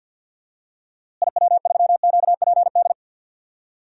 Morse code, heard as a keyed tone, about 46 wpm, 700 Hz. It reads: IW3QPD
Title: DXCC-100 Call Signs - 6 Characters - Repeated 3x - 50wpm